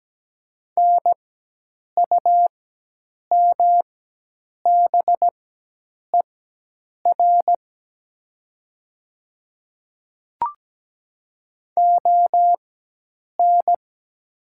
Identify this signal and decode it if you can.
Morse code, heard as a keyed tone, about 17 wpm, spaced as for 8 wpm, 700 Hz. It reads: NUMBER ON
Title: 4x Character Spacing - Top 100 Words - 17wpm